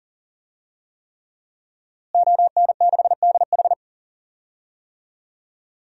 Morse code, heard as a keyed tone, about 40 wpm, 700 Hz. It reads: ON6DH
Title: DXCC-100 Call Signs - 5 Characters - Repeated 3x - 40wpm